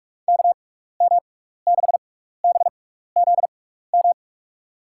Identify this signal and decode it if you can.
Morse code, heard as a keyed tone, about 46 wpm, spaced as for 16 wpm, 700 Hz. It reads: KM6BZM
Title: USA Call Signs - 6 Characters - 6x Character Spacing - 45wpm